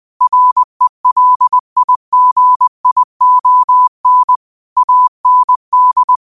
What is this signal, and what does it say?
Morse code, heard as a keyed tone, about 20 wpm, 1000 Hz. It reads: RELIGION AND
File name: qotdfiles / 20wpm